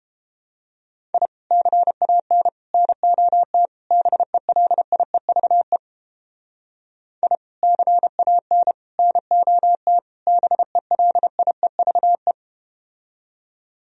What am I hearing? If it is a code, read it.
Morse code, heard as a keyed tone, about 32 wpm, 700 Hz. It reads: I CAN NOT BELIEVE I CAN NOT BELIEVE